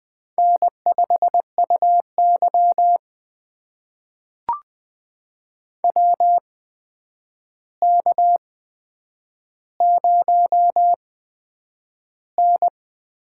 Morse code, heard as a keyed tone, about 20 wpm, spaced as for 9 wpm, 700 Hz. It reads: N5UY WK0N
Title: USA Call Signs - 4 Characters - 8x Character Spacing - 20wpm